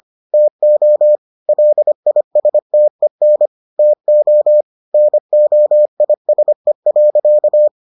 Morse code, heard as a keyed tone, about 25 wpm, 600 Hz. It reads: TO LISTEN TO NOISE.